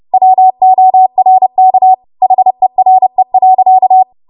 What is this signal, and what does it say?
Morse code, heard as a keyed tone, about 30 wpm, 750 Hz. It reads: WORK HERE.
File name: Archive / 30 WPM / 30WPM